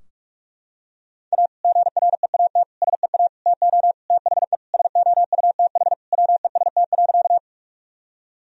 Morse code, heard as a keyed tone, about 41 wpm, 700 Hz. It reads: A GREAT SEA TO THE SOUTH WEST.
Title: Sentences from Top 1000 Words - Review Part 2 - 45wpm